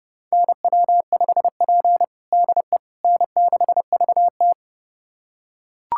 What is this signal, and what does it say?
Morse code, heard as a keyed tone, about 32 wpm, 700 Hz. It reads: NW5P DE N6VT
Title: USA Callsign DE Callsign - 4 Characters - 30wpm